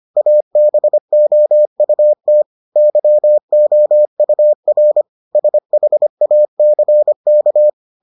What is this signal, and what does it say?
Morse code, heard as a keyed tone, about 25 wpm, 600 Hz. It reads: ABOUT YOUR SHACK